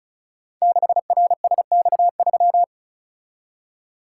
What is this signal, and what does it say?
Morse code, heard as a keyed tone, about 36 wpm, 700 Hz. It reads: 6RSX3
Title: Code Groups - 5 Letter:Numbers - 35wpm